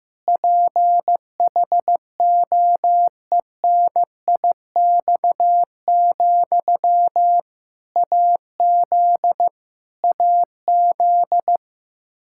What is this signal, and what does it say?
Morse code, heard as a keyed tone, about 15 wpm, 700 Hz. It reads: PHOENIX, AZ AZ
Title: K1USN SST QSO Practice 15wpm